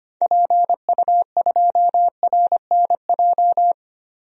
Morse code, heard as a keyed tone, about 25 wpm, 700 Hz. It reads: PU2RNJ
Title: DXCC-100 Call Signs - 6 Characters - 25wpm